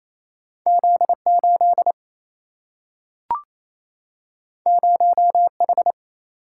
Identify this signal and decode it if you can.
Morse code, heard as a keyed tone, about 28 wpm, 700 Hz. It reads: Z 8 0 H
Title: Single-Letter Single-Number 28wpm